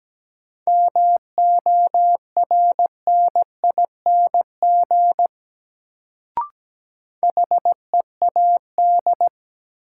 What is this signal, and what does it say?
Morse code, heard as a keyed tone, about 17 wpm, 700 Hz. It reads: MORNING HEAD
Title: Sets of 3 Words - New Word Drill - 17wpm